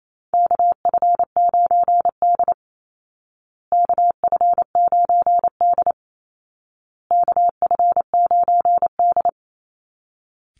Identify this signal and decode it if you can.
Morse code, heard as a keyed tone, about 28 wpm, 700 Hz. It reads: KF9D KF9D KF9D
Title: Call Signs Repeated 2x - 28wpm